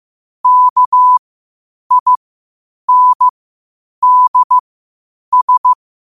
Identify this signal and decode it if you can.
Morse code, heard as a keyed tone, about 15 wpm, spaced as for 8 wpm, 1000 Hz. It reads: KINDS
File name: qotdfiles / 8wpm